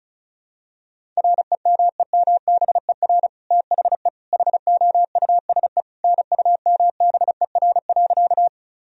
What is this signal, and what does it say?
Morse code, heard as a keyed tone, about 34 wpm, 700 Hz. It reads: REMEMBER THE HOUSE NUMBER.
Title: Sentences from Top 1000 Words - Review Part 1 - 35wpm